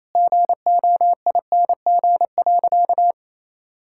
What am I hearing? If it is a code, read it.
Morse code, heard as a keyed tone, about 28 wpm, 700 Hz. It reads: GOING.
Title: Sentences from Top 1000 Words - Review Part 5 - 28wpm